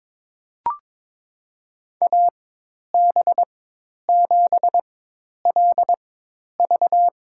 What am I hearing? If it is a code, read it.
Morse code, heard as a keyed tone, about 22 wpm, spaced as for 10 wpm, 700 Hz. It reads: AB7LV
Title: USA Call Signs - 4x Character Spacing - 22wpm